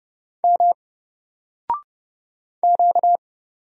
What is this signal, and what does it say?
Morse code, heard as a keyed tone, about 31 wpm, 700 Hz. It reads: M Q
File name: Single-Letter-Number-30wpm